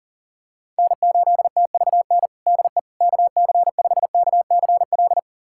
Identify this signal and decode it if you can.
Morse code, heard as a keyed tone, about 37 wpm, 700 Hz. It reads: N8TVN DE KC5KCL
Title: USA Callsign DE Callsign 40wpm